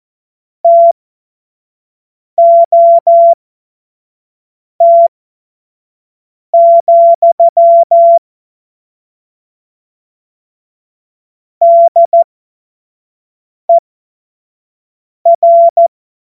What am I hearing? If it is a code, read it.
Morse code, heard as a keyed tone, about 14 wpm, spaced as for 5 wpm, 680 Hz. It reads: TOT, DER
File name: Morsekurs / Witze / Witz 02 Januar 2024 25 F70